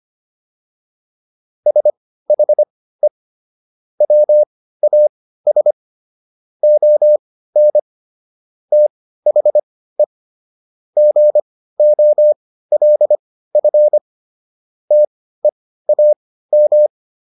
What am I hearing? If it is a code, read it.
Morse code, heard as a keyed tone, about 25 wpm, spaced as for 15 wpm, 600 Hz. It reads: SHE WAS ON THE GOLF TEAM